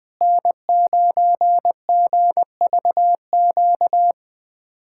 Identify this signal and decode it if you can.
Morse code, heard as a keyed tone, about 21 wpm, 700 Hz. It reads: N9GVQ
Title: USA Callsign DE Callsign 20wpm